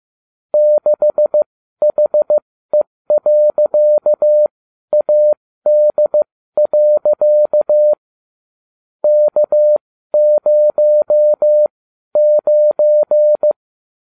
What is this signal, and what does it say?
Morse code, heard as a keyed tone, about 15 wpm, spaced as for 12 wpm, 600 Hz. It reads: 6HE.AD. K09